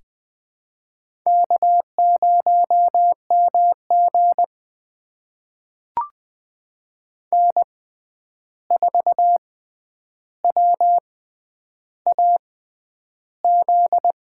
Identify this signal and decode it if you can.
Morse code, heard as a keyed tone, about 21 wpm, spaced as for 10 wpm, 700 Hz. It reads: K0MG N4WAZ
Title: USA Call Signs - 6x Character Spacing - 20wpm